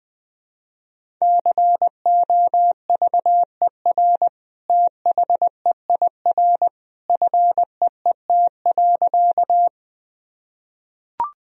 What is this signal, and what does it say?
Morse code, heard as a keyed tone, about 20 wpm, 700 Hz. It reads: COVER THEIR FEET.
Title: Sentences from Top 1000 Words - Review Part 1 - 20wpm